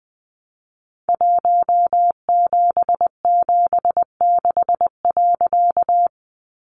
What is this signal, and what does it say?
Morse code, heard as a keyed tone, about 20 wpm, 700 Hz. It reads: 1776.
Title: A Sound Of Thunder 20wpm